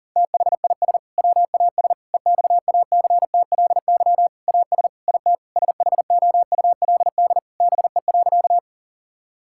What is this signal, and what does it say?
Morse code, heard as a keyed tone, about 37 wpm, 700 Hz. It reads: THIS WAS EXACTLY AS IT SHOULD BE.
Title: Sentences from Top 1000 Words - Review Part 6 - 40wpm